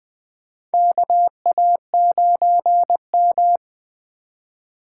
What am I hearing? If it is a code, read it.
Morse code, heard as a keyed tone, about 20 wpm, 700 Hz. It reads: KA9M
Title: USA Call Signs - 4 Characters - 2x Character Spacing - 20wpm